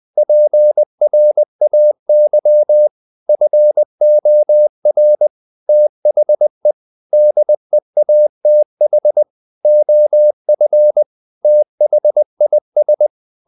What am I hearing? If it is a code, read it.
Morse code, heard as a keyed tone, about 20 wpm, 600 Hz. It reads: PRAY FOR THE DEATH OF THIS
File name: the old man and the sea / the old man and the sea part 1 20wpm